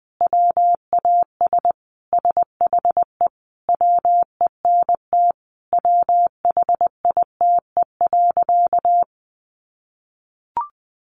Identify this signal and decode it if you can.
Morse code, heard as a keyed tone, about 20 wpm, 700 Hz. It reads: WAS SHE WENT WHITE.
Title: Sentences from Top 1200 Words 20wpm